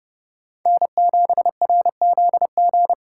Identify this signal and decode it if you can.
Morse code, heard as a keyed tone, about 32 wpm, 700 Hz. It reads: N7RZG